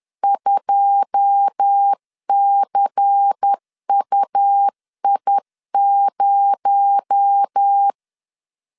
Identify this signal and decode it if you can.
Morse code, heard as a keyed tone, about 10 wpm, 800 Hz. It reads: 2CUI0
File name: LEZIONI / 9 2 10wpm 2736